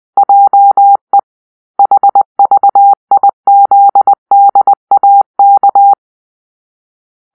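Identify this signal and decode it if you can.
Morse code, heard as a keyed tone, about 22 wpm, 800 Hz. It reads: JE HVIZDAK
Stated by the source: Morse-code cipher made by a player team for the Kachna puzzle game.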